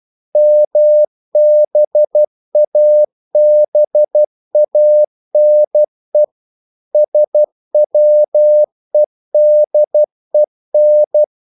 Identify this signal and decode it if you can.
Morse code, heard as a keyed tone, about 12 wpm, 600 Hz. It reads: MBABANE SWEDEN
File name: CW / Practice / NationalCapitals12